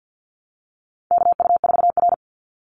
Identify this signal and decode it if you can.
Morse code, heard as a keyed tone, about 50 wpm, 700 Hz. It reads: KU4R
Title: NA SOTA Chaser Callsigns 50wpm